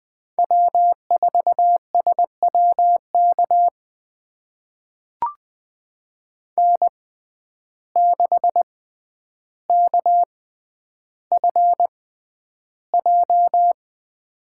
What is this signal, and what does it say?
Morse code, heard as a keyed tone, about 21 wpm, spaced as for 10 wpm, 700 Hz. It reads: W4SWK N6KFJ
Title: USA Call Signs - 5 Characters - 6x Character Spacing - 20wpm